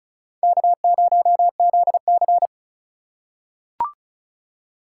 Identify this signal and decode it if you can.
Morse code, heard as a keyed tone, about 36 wpm, 700 Hz. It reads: K0ZC
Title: USA Call Signs - 4 Characters - 4x Character Spacing - 35wpm